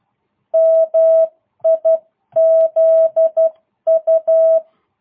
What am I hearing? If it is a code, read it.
Morse code, heard as a keyed tone, about 12 wpm, 660 Hz. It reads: MIZU